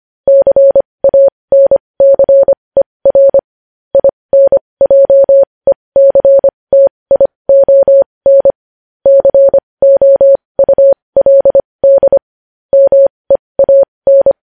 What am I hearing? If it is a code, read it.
Morse code, heard as a keyed tone, about 25 wpm, spaced as for 20 wpm, 550 Hz. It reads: CANCER INJECTION COULD MEAN
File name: podcast / News headlines-1x-25x20